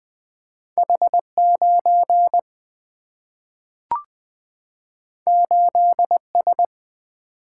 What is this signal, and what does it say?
Morse code, heard as a keyed tone, about 20 wpm, 700 Hz. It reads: H 9 8 S
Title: Single-Letter Single-Number 20wpm